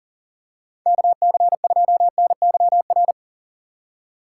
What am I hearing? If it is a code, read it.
Morse code, heard as a keyed tone, about 40 wpm, 700 Hz. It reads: KC2NYR
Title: USA Call Signs - 2x Character Spacing - 40wpm